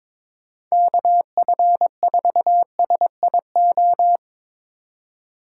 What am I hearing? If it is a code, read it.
Morse code, heard as a keyed tone, about 22 wpm, 700 Hz. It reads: KF4SIO
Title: USA Call Signs - 6 Characters - 3x Character Spacing - 22wpm